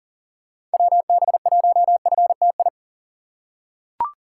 Morse code, heard as a keyed tone, about 37 wpm, 700 Hz. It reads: WB1FTI
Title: USA Call Signs - 6 Characters - 2x Character Spacing - 40wpm